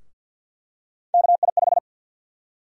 Morse code, heard as a keyed tone, about 51 wpm, 700 Hz. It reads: KI5
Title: USA Call Sign Prefixes - Repeated 2x - 50wpm